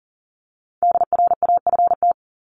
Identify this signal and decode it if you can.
Morse code, heard as a keyed tone, about 40 wpm, 700 Hz. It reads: DRAFT
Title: Top 1751-2000 Words 40wpm